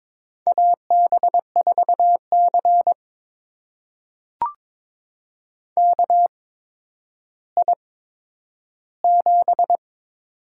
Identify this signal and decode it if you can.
Morse code, heard as a keyed tone, about 22 wpm, spaced as for 11 wpm, 700 Hz. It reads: AB4C KI7
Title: USA Call Signs - 8x Character Spacing - 22wpm